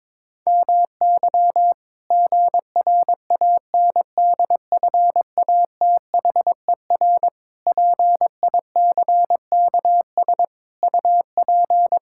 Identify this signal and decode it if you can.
Morse code, heard as a keyed tone, about 22 wpm, 700 Hz. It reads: MY GRANDFATHER PICKS UP
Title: Pangram Challenge v2 - 22wpm